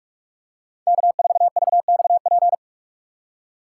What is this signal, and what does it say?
Morse code, heard as a keyed tone, about 46 wpm, 700 Hz. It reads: K4VXP
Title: USA Call Signs - 3x Character Spacing - 45wpm